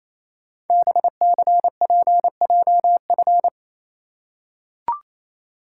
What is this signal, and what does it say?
Morse code, heard as a keyed tone, about 29 wpm, 700 Hz. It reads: BCPJF
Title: Code Groups - 5 Letters - 28wpm